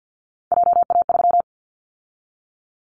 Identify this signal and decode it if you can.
Morse code, heard as a keyed tone, about 53 wpm, 700 Hz. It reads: JA3
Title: Callsign Prefixes for Top 50 DXCC - 50wpm